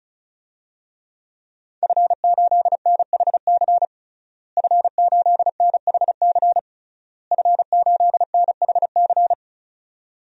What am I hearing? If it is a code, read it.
Morse code, heard as a keyed tone, about 36 wpm, 700 Hz. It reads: F8NHC F8NHC F8NHC
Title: DXCC-100 Call Signs - 5 Characters - Repeated 3x - 35wpm